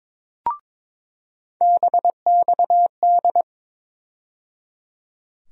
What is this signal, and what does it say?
Morse code, heard as a keyed tone, about 21 wpm, 700 Hz. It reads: BXD
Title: USA Call Sign Postfixes - 3 Letters 22wpm